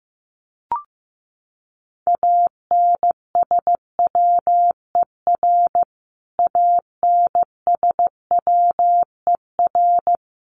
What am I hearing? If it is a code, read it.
Morse code, heard as a keyed tone, about 15 wpm, 700 Hz. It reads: ANSWER ANSWER
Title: Top 300 Words Repeated 15wpm